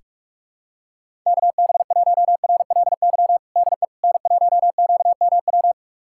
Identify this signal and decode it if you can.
Morse code, heard as a keyed tone, about 46 wpm, 700 Hz. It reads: KB1RLY DE N1QMW
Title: USA Callsign DE Callsign 45wpm